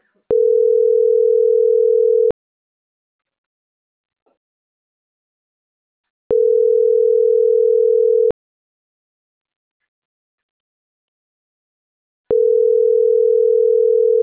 Ringback tone (North America), 440 + 480 Hz.